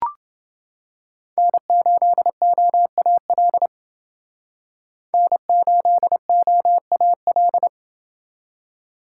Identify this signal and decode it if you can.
Morse code, heard as a keyed tone, about 28 wpm, 700 Hz. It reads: N8OAL N8OAL